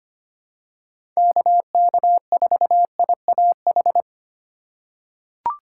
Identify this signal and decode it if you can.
Morse code, heard as a keyed tone, about 25 wpm, 700 Hz. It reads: KK4IAH
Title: USA Call Signs - 3x Character Spacing - 25wpm